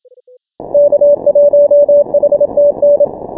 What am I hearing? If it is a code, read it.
Morse code, heard as a keyed tone, about 26 wpm, 590 Hz. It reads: K1HTN
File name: final / W1UJ-168-140-MorseRunner